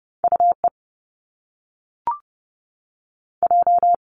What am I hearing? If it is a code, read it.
Morse code, heard as a keyed tone, about 30 wpm, 700 Hz. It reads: U E J
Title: Top Two Letter Combinations in Words 30wpm